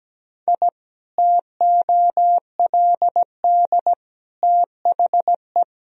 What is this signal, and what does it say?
Morse code, heard as a keyed tone, about 17 wpm, 700 Hz. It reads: I TOLD THE
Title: Sentences from Top 1000 Words - Review Part 4 - 17wpm